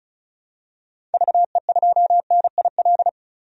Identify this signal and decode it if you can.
Morse code, heard as a keyed tone, about 36 wpm, 700 Hz. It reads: VE2NIL
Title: DXCC-100 Call Signs - 6 Characters - 35wpm